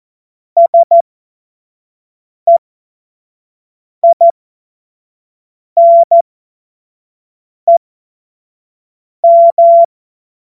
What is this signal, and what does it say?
Morse code, heard as a keyed tone, about 14 wpm, spaced as for 5 wpm, 680 Hz. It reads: SEINEM
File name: Witz-05-August-2025-25-F70